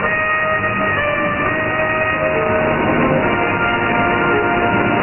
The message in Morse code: none